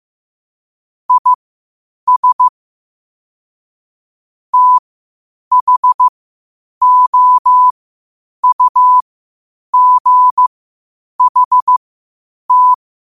Morse code, heard as a keyed tone, about 15 wpm, spaced as for 8 wpm, 1000 Hz. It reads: IS THOUGHT
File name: qotdfiles / 8wpm